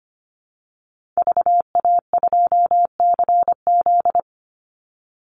Morse code, heard as a keyed tone, about 25 wpm, 700 Hz. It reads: VA2CZ
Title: Call Signs Repeated 3x - 25wpm